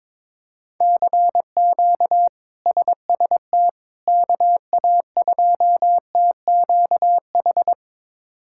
Morse code, heard as a keyed tone, about 22 wpm, 700 Hz. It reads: CQ SST KA2TQH
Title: K1USN SST QSO Practice - 22wpm